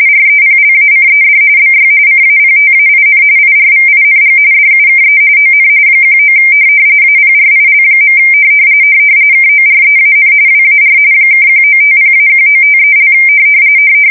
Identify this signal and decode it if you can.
RTTY, about 45.45 baud. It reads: REPLACED BY CAPTAIN IRVING DUKE ON 19 APRIL. KOREAN WAR (1950-1953)
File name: fsk170 / USS-Missouri